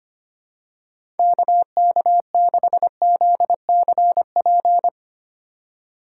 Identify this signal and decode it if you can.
Morse code, heard as a keyed tone, about 25 wpm, 700 Hz. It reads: KK6ZCP
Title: USA Call Signs - 6 Characters - 4x Character Spacing - 25wpm